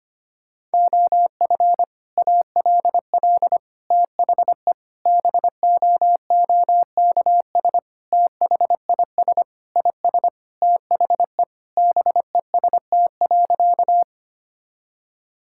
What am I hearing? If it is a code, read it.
Morse code, heard as a keyed tone, about 25 wpm, 700 Hz. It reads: OF ALL THE BOOKS THIS IS THE BEST.
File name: practice / 25wpm / Sentences with 3-Word Phrases - 25wpm